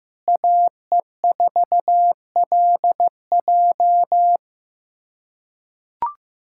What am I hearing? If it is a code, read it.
Morse code, heard as a keyed tone, about 15 wpm, 700 Hz. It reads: AE4LJ
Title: USA Callsign DE Callsign - 5 Characters - 15wpm